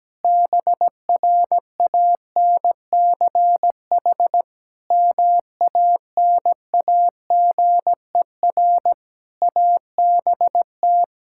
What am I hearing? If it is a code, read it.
Morse code, heard as a keyed tone, about 17 wpm, 700 Hz. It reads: BRANCH MANAGER ABT
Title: Practice QSO - Exchange 3 - More Info - 17wpm